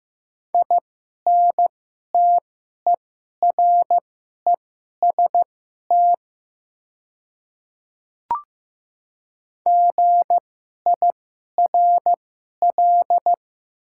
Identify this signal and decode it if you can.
Morse code, heard as a keyed tone, about 15 wpm, spaced as for 11 wpm, 700 Hz. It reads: INTEREST GIRL
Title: 2x Character Spacing - Top 500 Words - 15wpm